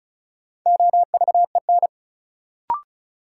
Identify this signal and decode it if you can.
Morse code, heard as a keyed tone, about 36 wpm, 700 Hz. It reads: OVEN
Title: Sentences from Top 1000 Words - Review Part 4 - New Word Drill - 35wpm